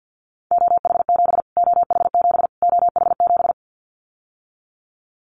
Morse code, heard as a keyed tone, about 50 wpm, 700 Hz. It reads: OH7 OH7 OH7
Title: Callsign Prefixes for Top 25 DXCC - Repeated 3x - 50wpm